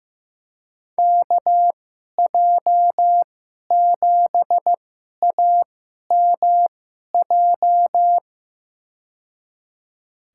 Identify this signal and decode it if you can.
Morse code, heard as a keyed tone, about 15 wpm, spaced as for 11 wpm, 700 Hz. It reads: KJ7AMJ
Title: USA Call Signs - 6 Characters - 2x Character Spacing - 15wpm